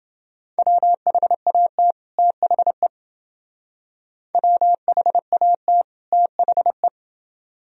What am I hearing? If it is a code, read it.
Morse code, heard as a keyed tone, about 28 wpm, 700 Hz. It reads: WHAT THE WHAT THE